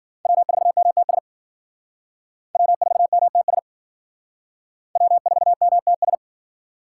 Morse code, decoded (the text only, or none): W4MTS W4MTS W4MTS